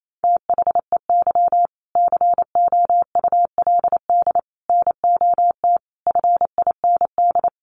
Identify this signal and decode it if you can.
Morse code, heard as a keyed tone, about 28 wpm, 700 Hz. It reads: THEY COULD NOT FIND
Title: Sentences from Top 1400 Words 28wpm